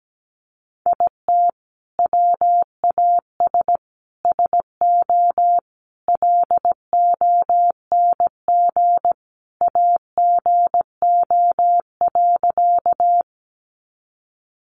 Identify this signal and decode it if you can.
Morse code, heard as a keyed tone, about 17 wpm, 700 Hz. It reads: IT WAS SO LONG AGO.
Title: Sentences from Top 500 Words 17wpm